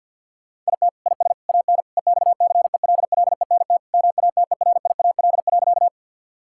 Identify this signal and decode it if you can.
Morse code, heard as a keyed tone, about 50 wpm, 700 Hz. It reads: IT IS AN EXCELLENT MATERIAL.
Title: Sentences from Top 1000 Words - Review Part 5 - 50wpm